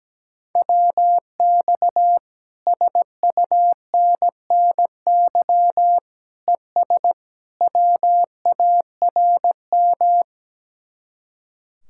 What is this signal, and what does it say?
Morse code, heard as a keyed tone, about 17 wpm, 700 Hz. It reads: WX SUNNY ES WARM
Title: Speed-Racing Common QSO Phrases - 17wpm